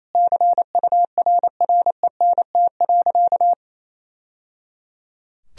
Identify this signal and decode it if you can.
Morse code, heard as a keyed tone, about 28 wpm, 700 Hz. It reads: CURRENT.
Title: Sentences from Top 1000 Words - Review Part 4 - 28wpm